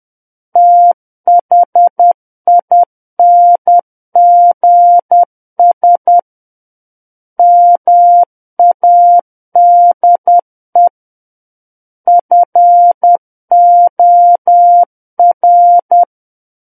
Morse code, heard as a keyed tone, about 10 wpm, 700 Hz. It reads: THINGS MADE FOR
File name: Book One / 01-Chapter 1 - 10 WPM